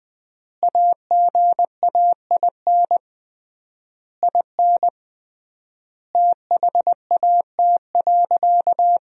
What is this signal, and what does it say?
Morse code, heard as a keyed tone, about 21 wpm, 700 Hz. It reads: AGAIN IN THAT.
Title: 3x Word-Spacing - Sentences from Top 200 Words - 20wpm